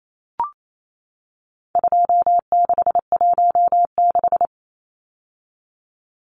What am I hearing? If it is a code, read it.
Morse code, heard as a keyed tone, about 28 wpm, 700 Hz. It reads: 2616
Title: Numbers - 4-Digit 28wpm